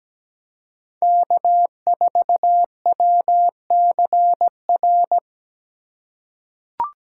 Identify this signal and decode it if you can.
Morse code, heard as a keyed tone, about 17 wpm, 700 Hz. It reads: K4WCR
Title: USA Call Signs - 3x Character Spacing - 17wpm